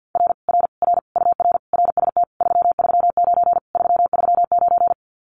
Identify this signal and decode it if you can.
Morse code, heard as a keyed tone, about 50 wpm, 700 Hz. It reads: R R R UR RST 339 339
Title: Common QSO Phrases Encore 50wpm